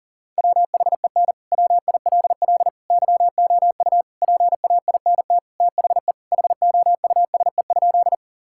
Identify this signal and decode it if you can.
Morse code, heard as a keyed tone, about 40 wpm, 700 Hz. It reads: WHEN WILL YOU PAINT THE HOUSE?
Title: Sentences from Top 1000 Words - Review Part 6 - 40wpm